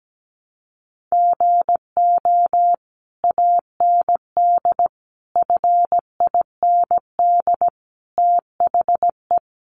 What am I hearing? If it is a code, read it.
Morse code, heard as a keyed tone, about 17 wpm, 700 Hz. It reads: GO AND FIND THE
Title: Sentences from Top 1200 Words 17wpm